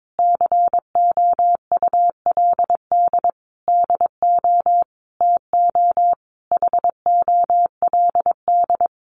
Morse code, heard as a keyed tone, about 22 wpm, 700 Hz. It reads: COULD DO TO HOLD
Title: Sentences from Top 400 Words 22wpm